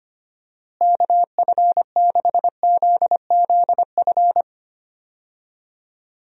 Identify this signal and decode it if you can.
Morse code, heard as a keyed tone, about 25 wpm, 700 Hz. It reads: KF6ZZF
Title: USA Call Signs - 6 Characters - Repeated 3x - 25wpm